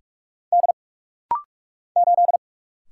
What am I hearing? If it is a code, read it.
Morse code, heard as a keyed tone, about 46 wpm, 700 Hz. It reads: D 8
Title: Single Letter-Number Alt - Rapid-Fire 45wpm